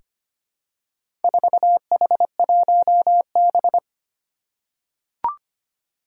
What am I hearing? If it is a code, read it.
Morse code, heard as a keyed tone, about 25 wpm, 700 Hz. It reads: 4H1B
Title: Four Challenging Letter-Number Combinations 25wpm